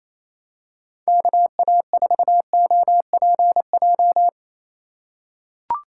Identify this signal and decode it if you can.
Morse code, heard as a keyed tone, about 27 wpm, 700 Hz. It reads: KA4OPJ
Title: USA Call Signs - 3x Character Spacing - 28wpm